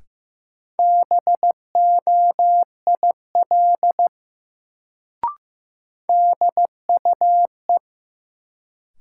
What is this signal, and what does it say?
Morse code, heard as a keyed tone, about 15 wpm, 700 Hz. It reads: BOIL DUE
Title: Sets of 5 Words Encore - New Word Drill - 15wpm